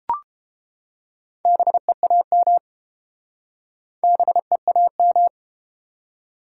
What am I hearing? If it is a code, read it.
Morse code, heard as a keyed tone, about 30 wpm, 700 Hz. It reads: BEAM BEAM